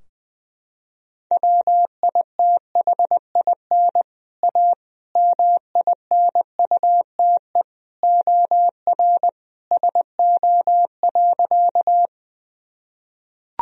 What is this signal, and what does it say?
Morse code, heard as a keyed tone, about 21 wpm, 700 Hz. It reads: WITHIN A MINUTE OR SO.
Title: Sentences from Top 1000 Words - Review Part 4 - 20wpm